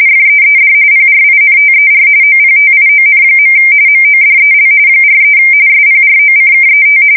RTTY, about 45.45 baud. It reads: FEED PRINTER OPTION. MODEL 43 SPROCKET
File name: fsk170 / teletype-corp